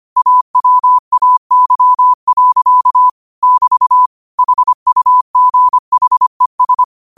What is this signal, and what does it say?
Morse code, heard as a keyed tone, about 25 wpm, 1000 Hz. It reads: AWAY. <BT> HUGHES
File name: qotdfiles / 25wpm